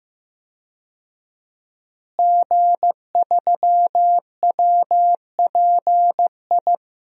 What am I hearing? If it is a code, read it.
Morse code, heard as a keyed tone, about 15 wpm, 700 Hz. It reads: G3WPI
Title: DXCC-100 Call Signs - 5 Characters - Repeated 2x - 15wpm